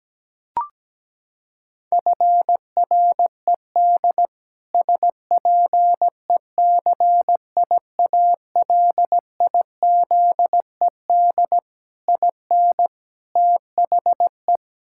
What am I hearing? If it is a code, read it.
Morse code, heard as a keyed tone, about 17 wpm, 700 Hz. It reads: FRED SPECIALIZED IN THE
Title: Pangram Challenge v2 - 17wpm